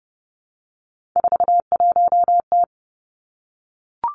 Morse code, heard as a keyed tone, about 30 wpm, 700 Hz. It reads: 41T
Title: Three Challenging Letter-Number Combinations 30wpm